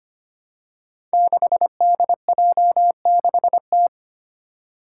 Morse code, heard as a keyed tone, about 25 wpm, 700 Hz. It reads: 6DJ6T
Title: Five Challenging Letter-Number Combinations 25wpm